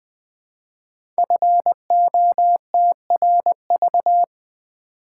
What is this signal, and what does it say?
Morse code, heard as a keyed tone, about 20 wpm, 700 Hz. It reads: FOTRV
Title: Code Groups - 5 Letters - 20wpm